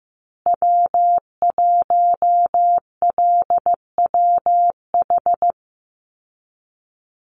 Morse code, heard as a keyed tone, about 15 wpm, 700 Hz. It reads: W1LWH
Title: Call Signs Repeated 2x - 15wpm